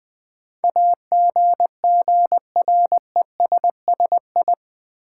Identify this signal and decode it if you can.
Morse code, heard as a keyed tone, about 20 wpm, 700 Hz. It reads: AGGRESSI
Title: Build A Word - 4 Letters At A Time - 10-Letter Words - 20wpm